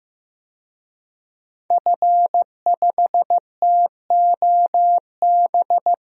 Morse code, heard as a keyed tone, about 15 wpm, 700 Hz. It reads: F5TOB
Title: DXCC-100 Call Signs - 5 Characters - Repeated 3x - 15wpm